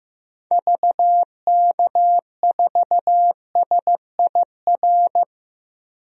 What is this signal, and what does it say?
Morse code, heard as a keyed tone, about 15 wpm, 700 Hz. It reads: VK4SIR
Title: DXCC-100 Call Signs - 6 Characters - 15wpm